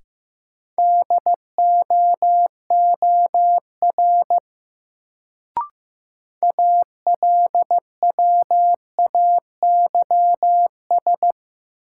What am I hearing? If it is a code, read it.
Morse code, heard as a keyed tone, about 15 wpm, 700 Hz. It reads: DOOR ALWAYS
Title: Sentences from Top 300 Words - New Word Drill - 15wpm